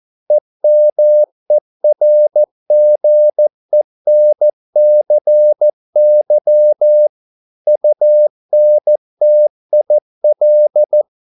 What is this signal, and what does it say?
Morse code, heard as a keyed tone, about 14 wpm, 600 Hz. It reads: EMERGENCY UNTIL